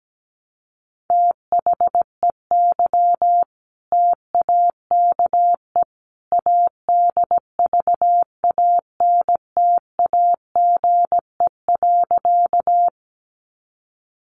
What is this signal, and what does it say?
Morse code, heard as a keyed tone, about 17 wpm, 700 Hz. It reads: THEY TAKE ADVANTAGE.
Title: Sentences from Top 900 Words 17wpm